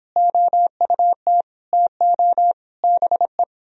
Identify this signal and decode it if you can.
Morse code, heard as a keyed tone, about 29 wpm, 700 Hz. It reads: OUT TO BE